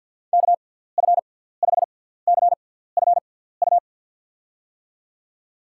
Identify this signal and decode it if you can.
Morse code, heard as a keyed tone, about 53 wpm, spaced as for 17 wpm, 700 Hz. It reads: KF5CFU
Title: USA Call Signs - 6 Characters - 6x Character Spacing - 50wpm